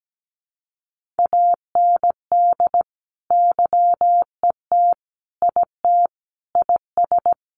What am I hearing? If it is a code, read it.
Morse code, heard as a keyed tone, about 17 wpm, 700 Hz. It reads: AND YET IT IS
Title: Sentences from Top 1500 Words 17wpm